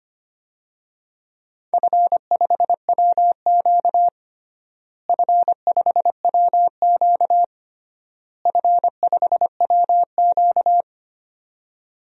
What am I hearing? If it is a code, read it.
Morse code, heard as a keyed tone, about 25 wpm, 700 Hz. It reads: F5WQ F5WQ F5WQ
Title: DXCC-100 Call Signs - 4 Characters - Repeated 3x - 25wpm